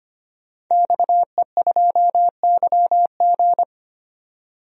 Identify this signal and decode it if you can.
Morse code, heard as a keyed tone, about 25 wpm, 700 Hz. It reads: XE2YG
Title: DXCC-100 Call Signs - 5 Characters - 25wpm